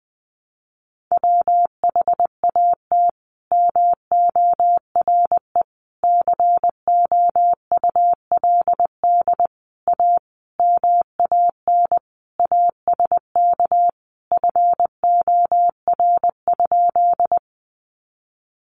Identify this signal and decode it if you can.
Morse code, heard as a keyed tone, about 20 wpm, 700 Hz. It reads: WHAT MORE COULD A MAN ASK FOR?
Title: Sentences from Top 200 Words 20wpm